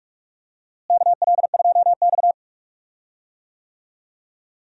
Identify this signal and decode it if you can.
Morse code, heard as a keyed tone, about 46 wpm, 700 Hz. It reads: KL2X
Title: USA Call Signs - 4 Characters - Repeated 3x - 45wpm